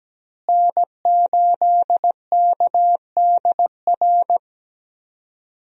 Morse code, heard as a keyed tone, about 17 wpm, 700 Hz. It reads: N8KDR
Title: USA Call Signs - 5 Characters - Repeated 3x - 17wpm